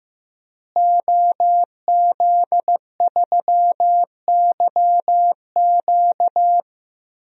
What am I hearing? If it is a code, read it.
Morse code, heard as a keyed tone, about 15 wpm, 700 Hz. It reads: OZ3YQ
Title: DXCC-100 Call Signs - 5 Characters - Repeated 2x - 15wpm